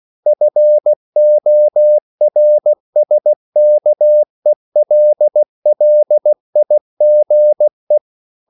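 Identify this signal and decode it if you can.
Morse code, heard as a keyed tone, about 16 wpm, 600 Hz. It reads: FORSKELLIGE
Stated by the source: Slow (QRS) Morse code at 16 wpm, from CW med Gnister.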